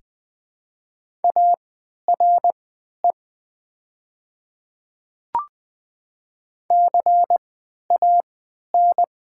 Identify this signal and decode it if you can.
Morse code, heard as a keyed tone, about 20 wpm, spaced as for 11 wpm, 700 Hz. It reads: ARE CAN
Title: 3x Character Spacing - Top 500 Words - 20wpm